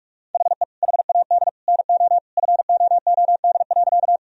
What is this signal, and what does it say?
Morse code, heard as a keyed tone, about 43 wpm, 700 Hz. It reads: HE HAD NO FOOD.
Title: Sentences from Top 1000 Words - Review Part 1 - 45wpm